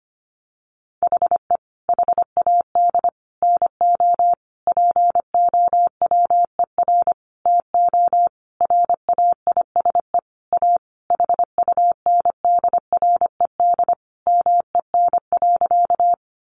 Morse code, heard as a keyed tone, about 25 wpm, 700 Hz. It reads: HE HAD NO POWER TO RAISE A HUNDRED MEN.
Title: Sentences from Top 1400 Words 25wpm